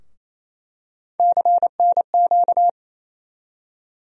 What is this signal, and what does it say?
Morse code, heard as a keyed tone, about 28 wpm, 700 Hz. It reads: CNQ
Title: USA Call Sign Postfixes - 3 Letters 28wpm